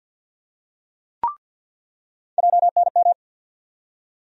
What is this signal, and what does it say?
Morse code, heard as a keyed tone, about 47 wpm, 700 Hz. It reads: JNM
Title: USA Call Sign Postfixes - 3 Letters 50wpm